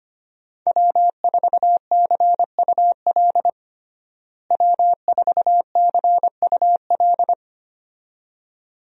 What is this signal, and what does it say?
Morse code, heard as a keyed tone, about 25 wpm, 700 Hz. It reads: W4CUL W4CUL
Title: USA Call Signs - 5 Characters - Repeated 2x - 25wpm